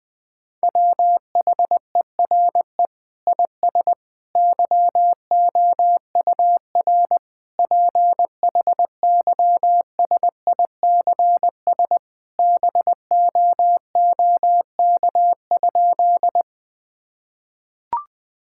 Morse code, heard as a keyed tone, about 20 wpm, 700 Hz. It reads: WHERE IS YOUR PHYSICS BOOK?
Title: Sentences from Top 1000 Words - Review Part 5 - 20wpm